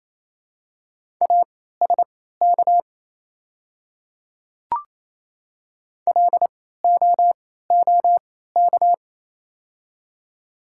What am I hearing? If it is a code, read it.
Morse code, heard as a keyed tone, about 27 wpm, spaced as for 16 wpm, 700 Hz. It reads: ASK LOOK
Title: 3x Character Spacing - Top 300 Words - 28wpm